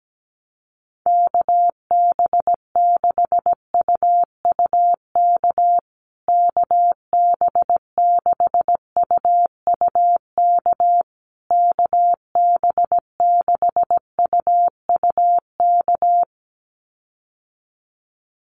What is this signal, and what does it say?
Morse code, heard as a keyed tone, about 17 wpm, 700 Hz. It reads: KB6UUK KB6UUK KB6UUK
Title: USA Call Signs Repeated 3x 17wpm